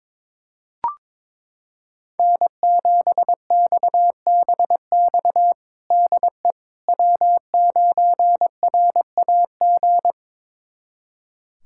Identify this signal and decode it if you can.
Morse code, heard as a keyed tone, about 23 wpm, 700 Hz. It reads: N7XBX DE W9RAG
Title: USA Callsign DE Callsign - 5 Characters - 22wpm